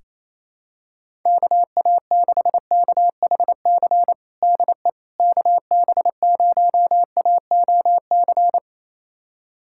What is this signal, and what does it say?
Morse code, heard as a keyed tone, about 28 wpm, 700 Hz. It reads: KA6KHC DE KB0AOC
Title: USA Callsign DE Callsign - 6 Characters - 28wpm